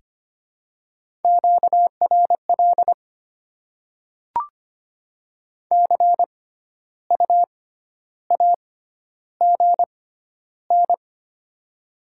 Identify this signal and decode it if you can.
Morse code, heard as a keyed tone, about 25 wpm, spaced as for 12 wpm, 700 Hz. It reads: QRL CU AGN
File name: Top 20 QSO Elements - 6x Character Spacing - 25wpm